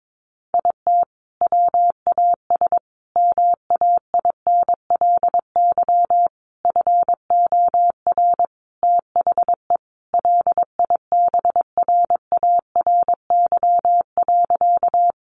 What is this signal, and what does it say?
Morse code, heard as a keyed tone, about 22 wpm, 700 Hz. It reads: IT WAS MAINLY FOR THE LIBRARY.
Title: Sentences from Top 1400 Words 22wpm